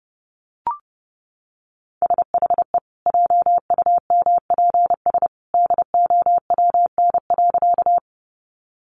Morse code, heard as a keyed tone, about 30 wpm, 700 Hz. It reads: SHE JUMPS DOWN.
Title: Easy Phrases 30wpm